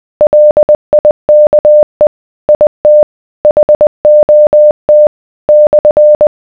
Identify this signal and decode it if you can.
Morse code, heard as a keyed tone, about 20 wpm, 600 Hz. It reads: LIKE IT HOT /